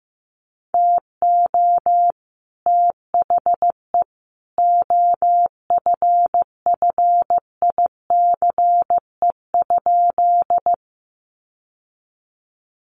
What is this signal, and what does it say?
Morse code, heard as a keyed tone, about 15 wpm, 700 Hz. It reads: TO THE OFFICE?
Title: Sentences from Top 1000 Words 15wpm